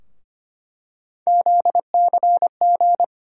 Morse code, heard as a keyed tone, about 25 wpm, 700 Hz. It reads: ZCG
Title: USA Call Sign Postfixes - 3 Letters 25wpm